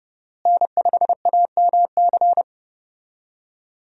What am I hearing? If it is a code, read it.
Morse code, heard as a keyed tone, about 30 wpm, 700 Hz. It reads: N5AMC
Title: USA Call Signs - 5 Characters - Repeated 2x - 30wpm